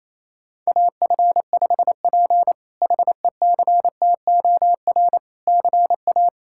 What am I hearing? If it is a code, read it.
Morse code, heard as a keyed tone, about 28 wpm, 700 Hz. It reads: AF5P HECTOR CA
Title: K1USN SST QSO Practice - 28wpm